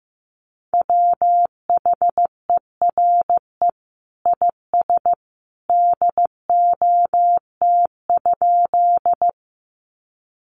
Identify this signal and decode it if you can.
Morse code, heard as a keyed tone, about 15 wpm, 700 Hz. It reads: WHERE IS DOT?
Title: Sentences from Top 1500 Words 15wpm